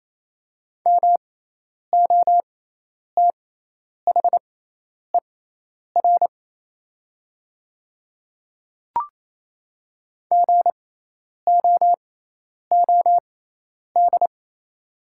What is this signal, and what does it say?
Morse code, heard as a keyed tone, about 28 wpm, spaced as for 10 wpm, 700 Hz. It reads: MOTHER GOOD
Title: 6x Character Spacing - Top 200 Words - 28wpm